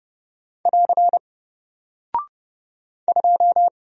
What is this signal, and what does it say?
Morse code, heard as a keyed tone, about 30 wpm, 700 Hz. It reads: <AR> 2
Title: Single Character (Letter:Number:Punctuation:Prosign) Alt 30wpm